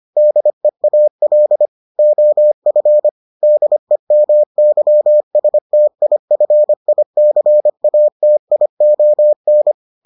Morse code, heard as a keyed tone, about 25 wpm, 600 Hz. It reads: DEAL OF DEMYSTIFICATION